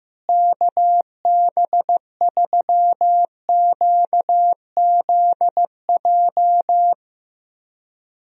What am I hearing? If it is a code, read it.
Morse code, heard as a keyed tone, about 15 wpm, 700 Hz. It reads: KB3QZJ
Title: USA Call Signs - 6 Characters - Repeated 3x - 15wpm